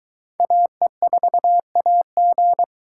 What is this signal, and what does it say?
Morse code, heard as a keyed tone, about 22 wpm, 700 Hz. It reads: AE4AG